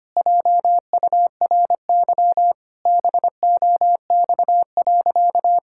Morse code, heard as a keyed tone, about 25 wpm, 700 Hz. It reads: JURY BOX.
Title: Pangram Challenge v2 - 25wpm